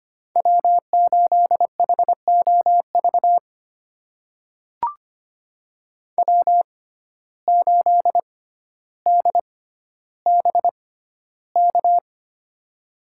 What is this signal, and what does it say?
Morse code, heard as a keyed tone, about 25 wpm, spaced as for 13 wpm, 700 Hz. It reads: W8HOV W8DBK
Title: USA Call Signs - 6x Character Spacing - 25wpm